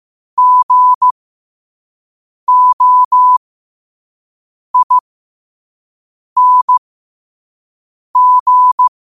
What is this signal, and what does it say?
Morse code, heard as a keyed tone, about 15 wpm, spaced as for 5 wpm, 1000 Hz. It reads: GOING
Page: 5 WPM morse code quotes for Wed, 30 Jul 2025.